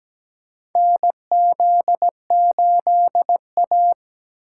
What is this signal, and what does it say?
Morse code, heard as a keyed tone, about 17 wpm, 700 Hz. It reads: NZ8A
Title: USA Call Signs - 8x Character Spacing - 17wpm